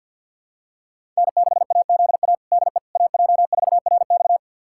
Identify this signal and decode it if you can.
Morse code, heard as a keyed tone, about 51 wpm, 700 Hz. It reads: N6AZA DE AJ4RX
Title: USA Callsign DE Callsign - 5 Characters - 50wpm